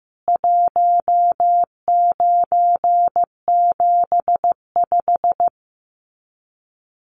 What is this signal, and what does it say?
Morse code, heard as a keyed tone, about 14 wpm, 700 Hz. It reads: 1975
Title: Common QSO Phrases 15wpm